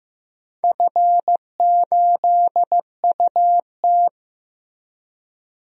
Morse code, heard as a keyed tone, about 15 wpm, 700 Hz. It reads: F8UT
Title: DXCC-100 Call Signs - 4 Characters - 15wpm